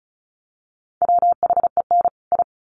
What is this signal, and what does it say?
Morse code, heard as a keyed tone, about 34 wpm, 700 Hz. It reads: WHEN I
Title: Sets of 2 Words 35wpm